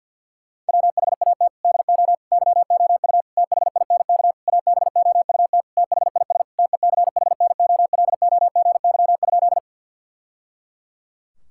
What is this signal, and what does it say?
Morse code, heard as a keyed tone, about 52 wpm, 690 Hz. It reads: WHAT DO YOU THINK ABOUT THIS TECHNOLOGY?
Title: Sentences from Top 1000 Words - Review Part 5 - 50wpm